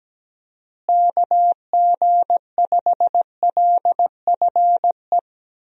Morse code, heard as a keyed tone, about 17 wpm, 700 Hz. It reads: KG5LFE
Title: USA Call Signs - 6 Characters - Repeated 3x - 17wpm